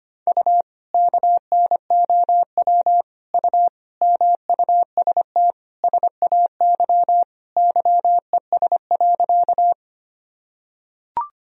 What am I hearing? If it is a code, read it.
Morse code, heard as a keyed tone, about 25 wpm, 700 Hz. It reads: U KNOW U MUST SAY YES.
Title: Sentences from Top 500 Words and Abbreviations 25wpm